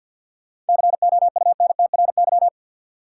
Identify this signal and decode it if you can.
Morse code, heard as a keyed tone, about 51 wpm, 700 Hz. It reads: COUNTRY
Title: Sentences from Top 300 Words - New Word Drill - 50wpm